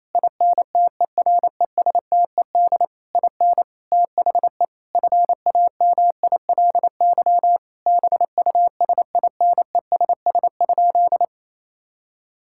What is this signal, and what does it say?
Morse code, heard as a keyed tone, about 28 wpm, 700 Hz. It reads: INTERESTED IN THE FAMILY BUSINESS?
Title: Sentences from Top 1000 Words - Review Part 6 - 28wpm